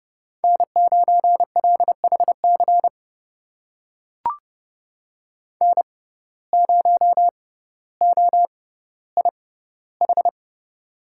Morse code, heard as a keyed tone, about 30 wpm, spaced as for 15 wpm, 700 Hz. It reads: N9LHC N0OIH
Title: USA Call Signs - 5 Characters - 6x Character Spacing - 30wpm